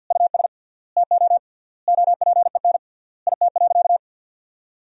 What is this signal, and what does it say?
Morse code, heard as a keyed tone, about 52 wpm, 700 Hz. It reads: US TO OPEN IT.
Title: 3x Word-Spacing - Sentences from Top 300 Words - 50wpm